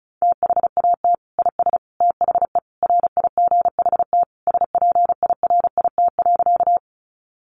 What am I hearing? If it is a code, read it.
Morse code, heard as a keyed tone, about 34 wpm, 700 Hz. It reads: THAT IS THE RIGHT SPIRIT.
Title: Sentences from Top 1400 Words 35wpm